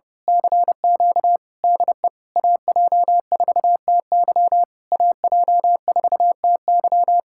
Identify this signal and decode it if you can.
Morse code, heard as a keyed tone, about 28 wpm, 700 Hz. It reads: CQ DE AJ4TY AJ4TY
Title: Speed-Racing Common QSO Phrases Encore - 30wpm